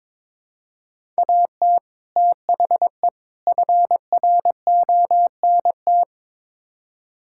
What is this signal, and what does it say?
Morse code, heard as a keyed tone, about 21 wpm, 700 Hz. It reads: AT THE FRONT
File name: practice / 22wpm / Build A Phrase - 5-Word Phrases - 22wpm